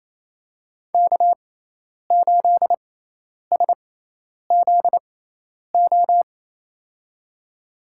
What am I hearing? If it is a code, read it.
Morse code, heard as a keyed tone, about 28 wpm, spaced as for 10 wpm, 700 Hz. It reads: K8SZO
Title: USA Call Signs - 5 Characters - 6x Character Spacing - 28wpm